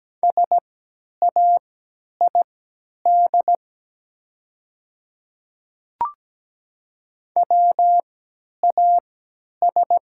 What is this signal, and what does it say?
Morse code, heard as a keyed tone, about 17 wpm, spaced as for 10 wpm, 700 Hz. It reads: SAID WAS
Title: 3x Character Spacing - Top 100 Words - 17wpm